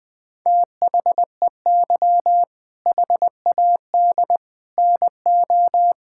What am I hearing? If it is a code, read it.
Morse code, heard as a keyed tone, about 19 wpm, 700 Hz. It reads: THEY HAD NO
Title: Sentences from Top 1000 Words - Review Part 6 - 20wpm